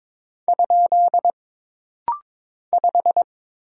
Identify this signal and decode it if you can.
Morse code, heard as a keyed tone, about 22 wpm, 700 Hz. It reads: ? 5
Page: This rapid-fire timing is geared to push your ICR (Instant Character Recognition) skills.